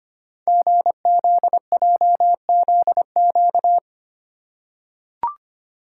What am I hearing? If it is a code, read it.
Morse code, heard as a keyed tone, about 25 wpm, 700 Hz. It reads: GZJZQ
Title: Code Groups - 5 Letter:Numbers - 25wpm